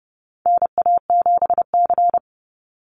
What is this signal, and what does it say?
Morse code, heard as a keyed tone, about 30 wpm, 700 Hz. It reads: NA7C
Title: Call Signs Repeated 2x - 30wpm